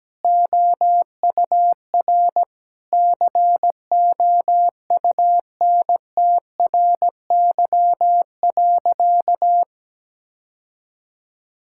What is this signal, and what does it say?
Morse code, heard as a keyed tone, about 17 wpm, 700 Hz. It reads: OUR COUNTRY.
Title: Sentences from Top 1000 Words - Review Part 6 - 17wpm